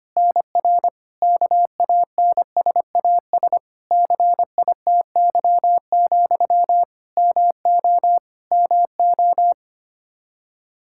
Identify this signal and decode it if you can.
Morse code, heard as a keyed tone, about 25 wpm, 700 Hz. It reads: NR KANSAS CITY, MO MO
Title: K1USN SST QSO Practice 25wpm